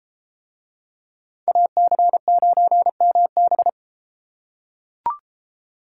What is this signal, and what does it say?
Morse code, heard as a keyed tone, about 33 wpm, 700 Hz. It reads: AC9MB